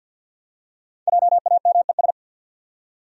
Morse code, heard as a keyed tone, about 52 wpm, 700 Hz. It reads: JAMES
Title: USA Common Names - Max 5 Letters - 50wpm